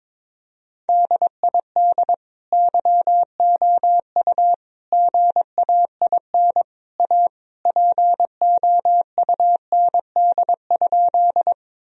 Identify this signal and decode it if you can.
Morse code, heard as a keyed tone, about 21 wpm, 700 Hz. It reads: DID YOU GAIN A POUND?
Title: Sentences from Top 1000 Words - Review Part 2 - 22wpm